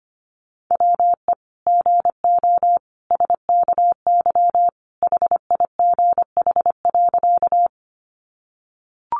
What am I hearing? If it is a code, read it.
Morse code, heard as a keyed tone, about 25 wpm, 700 Hz. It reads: WE GO SKY HIGH.
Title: Sentences from Top 1200 Words 25wpm